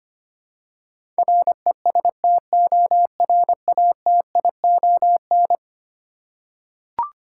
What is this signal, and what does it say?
Morse code, heard as a keyed tone, about 25 wpm, 700 Hz. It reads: RESTORATION
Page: morse-code-ninja-practice-25wpm